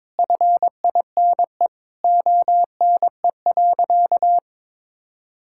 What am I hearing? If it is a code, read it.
Morse code, heard as a keyed tone, about 22 wpm, 700 Hz. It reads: FINE ONE.
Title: Sentences from Top 1000 Words - Review Part 6 - 22wpm